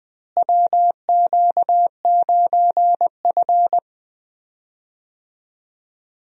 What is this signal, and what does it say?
Morse code, heard as a keyed tone, about 20 wpm, 700 Hz. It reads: WQ9F
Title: USA Call Signs - 4 Characters - Repeated 2x - 20wpm